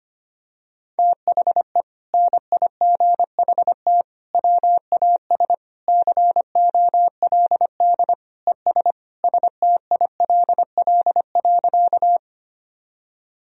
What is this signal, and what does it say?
Morse code, heard as a keyed tone, about 25 wpm, 700 Hz. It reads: THE NIGHT WAS COLD ES STILL.
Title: Sentences from Top 400 Words and Abbreviations 25wpm